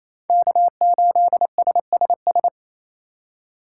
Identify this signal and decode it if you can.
Morse code, heard as a keyed tone, about 35 wpm, 700 Hz. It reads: K8SSS